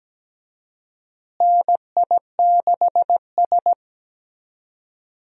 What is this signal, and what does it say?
Morse code, heard as a keyed tone, about 17 wpm, 700 Hz. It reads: NI6S
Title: USA Call Signs - 4 Characters - Repeated 3x - 17wpm